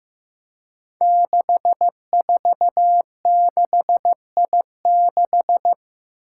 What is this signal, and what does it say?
Morse code, heard as a keyed tone, about 15 wpm, 700 Hz. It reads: 646I6
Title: Five Challenging Letter-Number Combinations 15wpm